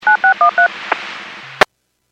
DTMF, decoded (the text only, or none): 9613